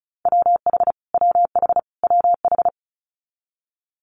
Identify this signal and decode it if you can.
Morse code, heard as a keyed tone, about 36 wpm, 700 Hz. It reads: WH WH WH
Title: Multi-letter Phonemes Repeated 3x 35wpm